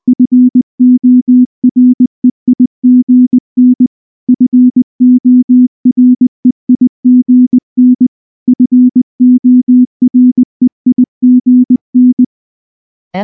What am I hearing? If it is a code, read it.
Morse code, heard as a keyed tone, about 19 wpm, 250 Hz. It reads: FOREIGN FOREIGN FOREIGN
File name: morse / morse-training-20WPM-1